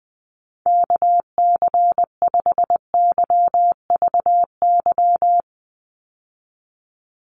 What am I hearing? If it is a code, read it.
Morse code, heard as a keyed tone, about 20 wpm, 700 Hz. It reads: KC5YVY
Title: USA Call Signs 20wpm